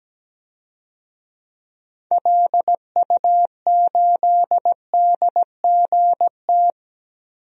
Morse code, heard as a keyed tone, about 17 wpm, 700 Hz. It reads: LU8DGT
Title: DXCC-100 Call Signs - 6 Characters - 17wpm